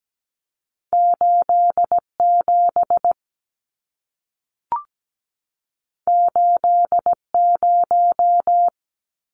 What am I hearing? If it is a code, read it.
Morse code, heard as a keyed tone, about 17 wpm, 700 Hz. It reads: 87 80
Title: Numbers - 2-Digit 17wpm